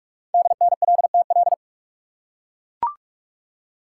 Morse code, heard as a keyed tone, about 44 wpm, 700 Hz. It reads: DNLTL
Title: Code Groups - 5 Letters - 45wpm